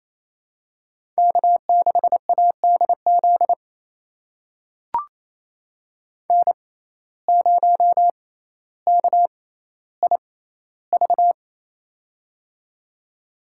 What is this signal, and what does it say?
Morse code, heard as a keyed tone, about 28 wpm, spaced as for 14 wpm, 700 Hz. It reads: K6ADZ N0KIV
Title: USA Call Signs - 5 Characters - 6x Character Spacing - 28wpm